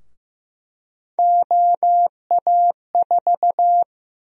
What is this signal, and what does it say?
Morse code, heard as a keyed tone, about 15 wpm, 700 Hz. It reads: OA4
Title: USA Call Sign Prefixes - 15wpm